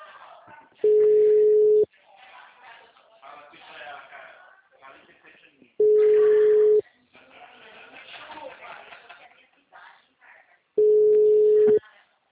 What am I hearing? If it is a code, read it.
Ringback tone (Europe, 425 Hz), with speech.